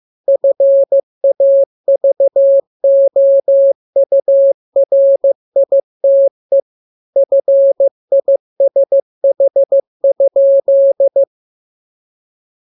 Morse code, heard as a keyed tone, about 15 wpm, 550 Hz. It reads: FAVOURITE FISH?